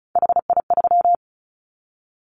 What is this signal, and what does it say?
Morse code, heard as a keyed tone, about 36 wpm, 700 Hz. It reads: HI3
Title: Callsign Prefixes for Top 100 DXCC - 35wpm